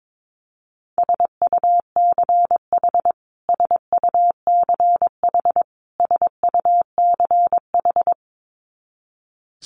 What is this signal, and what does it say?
Morse code, heard as a keyed tone, about 22 wpm, 700 Hz. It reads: SUCH SUCH SUCH
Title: Top 200 Words Repeated 22wpm